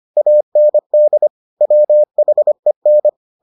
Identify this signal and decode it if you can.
Morse code, heard as a keyed tone, about 25 wpm, 600 Hz. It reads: AND WHEN